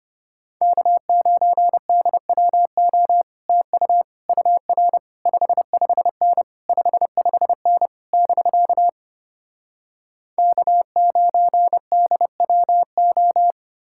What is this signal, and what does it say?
Morse code, heard as a keyed tone, about 33 wpm, 700 Hz. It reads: K9DWO TU UR 55N 55N <BK> K9DWO